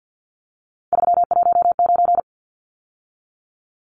Morse code, heard as a keyed tone, about 50 wpm, 700 Hz. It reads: 319
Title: Numbers - 3-Digit 50wpm